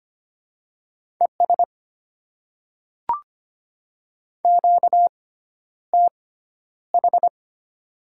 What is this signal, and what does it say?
Morse code, heard as a keyed tone, about 25 wpm, spaced as for 11 wpm, 690 Hz. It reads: ES QTH
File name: Top 90 QSO Elements - 6x Character Spacing - 25wpm